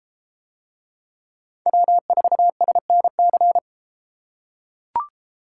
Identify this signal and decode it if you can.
Morse code, heard as a keyed tone, about 33 wpm, 700 Hz. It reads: W4SNC